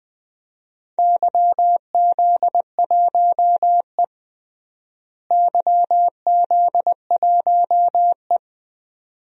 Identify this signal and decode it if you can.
Morse code, heard as a keyed tone, about 20 wpm, 700 Hz. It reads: YZ1E YZ1E
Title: DXCC-100 Call Signs - 4 Characters - Repeated 2x - 20wpm